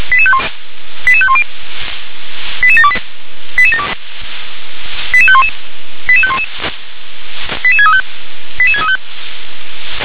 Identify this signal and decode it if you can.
Five-tone selective call (ZVEI), heard as quick runs of five tones, tones of about 70 ms. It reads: 88510 88728 88511 88511 88635 88635